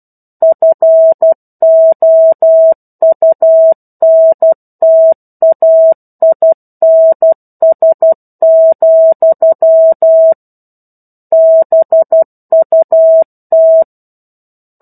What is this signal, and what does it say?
Morse code, heard as a keyed tone, about 12 wpm, 650 Hz. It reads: FOUNTAINS, BUT
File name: 01-Chapter-01 12 WPM